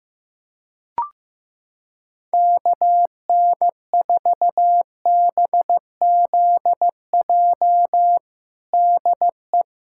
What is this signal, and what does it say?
Morse code, heard as a keyed tone, about 15 wpm, 700 Hz. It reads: KN4BZJ DE
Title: USA Callsign DE Callsign - 6 Characters - 15wpm